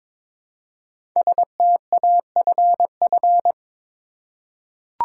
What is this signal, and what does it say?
Morse code, heard as a keyed tone, about 22 wpm, 700 Hz. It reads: STAFF
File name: 5 Letter Words - First 3 Letters Given - 22wpm